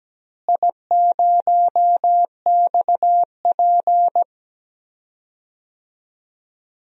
Morse code, heard as a keyed tone, about 17 wpm, 700 Hz. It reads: I0XP
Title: DXCC-100 Call Signs - 4 Characters - Repeated 3x - 17wpm